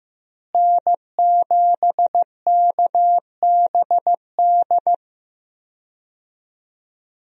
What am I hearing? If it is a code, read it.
Morse code, heard as a keyed tone, about 15 wpm, 700 Hz. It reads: N7KBD
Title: USA Call Signs - 5 Characters - Repeated 3x - 15wpm